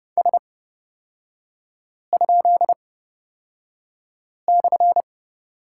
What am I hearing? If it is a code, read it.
Morse code, heard as a keyed tone, about 30 wpm, spaced as for 5 wpm, 700 Hz. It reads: S?/
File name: Lesson 36e - BT - Three Characters 30wpm